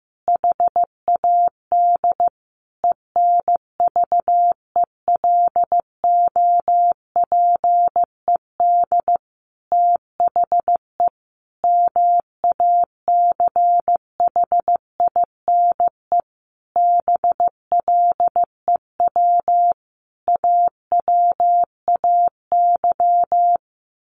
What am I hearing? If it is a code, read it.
Morse code, heard as a keyed tone, about 15 wpm, 700 Hz. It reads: HAD ENVELOPED THE MACHINE BLEW AWAY
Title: A Sound Of Thunder 15wpm